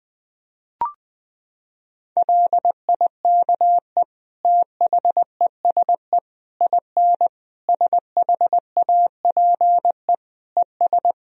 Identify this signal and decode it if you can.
Morse code, heard as a keyed tone, about 20 wpm, 700 Hz. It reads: LIKE THESE IN SHAPE ES
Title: Sentences from Top 500 Words and Abbreviations 20wpm